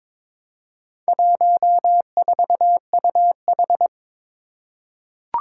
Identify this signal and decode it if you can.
Morse code, heard as a keyed tone, about 22 wpm, 700 Hz. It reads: 14UH
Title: Four Challenging Letter-Number Combinations 22wpm